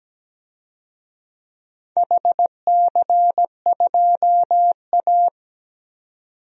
Morse code, heard as a keyed tone, about 17 wpm, 700 Hz. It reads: HC2A
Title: DXCC-100 Call Signs - 4 Characters - 17wpm